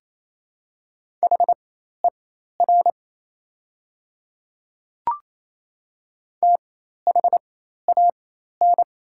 Morse code, heard as a keyed tone, about 28 wpm, spaced as for 13 wpm, 700 Hz. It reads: HER THAN
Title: 4x Character Spacing - Top 100 Words - 28wpm